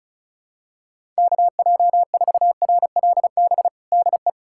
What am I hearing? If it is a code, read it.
Morse code, heard as a keyed tone, about 35 wpm, 700 Hz. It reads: KJ4RLB DE
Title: USA Callsign DE Callsign - 6 Characters - 35wpm